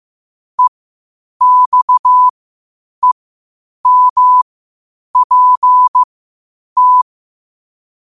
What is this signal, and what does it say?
Morse code, heard as a keyed tone, about 15 wpm, spaced as for 8 wpm, 1000 Hz. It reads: EXEMPT
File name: qotdfiles / 8wpm